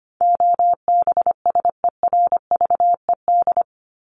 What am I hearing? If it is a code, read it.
Morse code, heard as a keyed tone, about 25 wpm, 700 Hz. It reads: OBSERVED
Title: ED Suffix - Top 100 Words with Base Words 25wpm